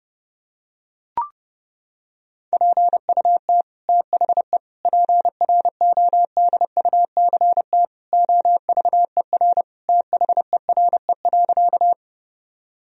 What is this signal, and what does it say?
Morse code, heard as a keyed tone, about 30 wpm, 700 Hz. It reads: PUT THE PRODUCT OVER THERE.
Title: Sentences from Top 1000 Words - Review Part 2 - 30wpm